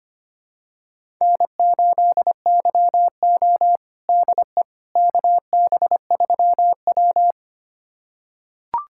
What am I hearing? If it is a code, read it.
Morse code, heard as a keyed tone, about 25 wpm, 700 Hz. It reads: N8YO DE KB3W
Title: USA Callsign DE Callsign - 4 Characters - 25wpm